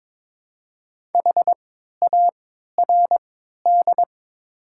Morse code, heard as a keyed tone, about 22 wpm, spaced as for 12 wpm, 700 Hz. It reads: HARD
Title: 3x Character Spacing - Top 500 Words - 22wpm